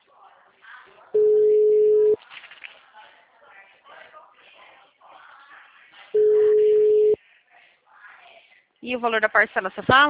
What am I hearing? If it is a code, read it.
Ringback tone (Europe, 425 Hz), with speech.
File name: out- -2015- - - .7988